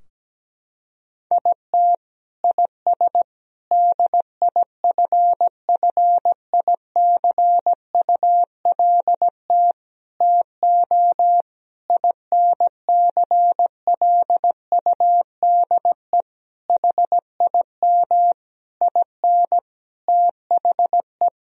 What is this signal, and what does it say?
Morse code, heard as a keyed tone, about 17 wpm, 700 Hz. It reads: IT IS DIFFICULT TO INCLUDE HIM IN THE
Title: Sentences from Top 1000 Words - Review Part 3 - 17wpm